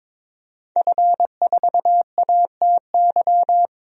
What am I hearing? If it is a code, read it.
Morse code, heard as a keyed tone, about 22 wpm, 700 Hz. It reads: F4ATY
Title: DXCC-100 Call Signs - 5 Characters - 22wpm